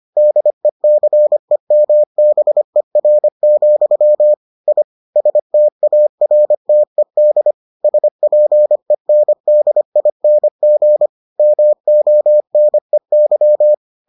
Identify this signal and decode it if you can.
Morse code, heard as a keyed tone, about 25 wpm, 600 Hz. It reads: DECEMBER, I STARTED SPENDING MONEY